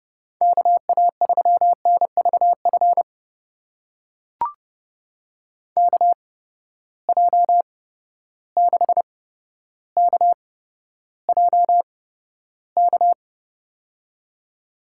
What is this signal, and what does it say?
Morse code, heard as a keyed tone, about 32 wpm, spaced as for 13 wpm, 700 Hz. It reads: KA3NVF KJ6KJK
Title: USA Call Signs - 6 Characters - 8x Character Spacing - 30wpm